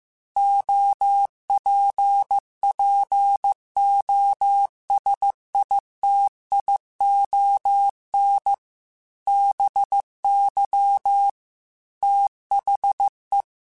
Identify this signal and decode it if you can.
Morse code, heard as a keyed tone, about 15 wpm, 800 Hz. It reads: OPPOSITION BY THE